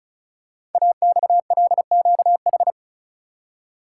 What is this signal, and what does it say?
Morse code, heard as a keyed tone, about 36 wpm, 700 Hz. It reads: AXLQH
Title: Code Groups - 5 Letters - 35wpm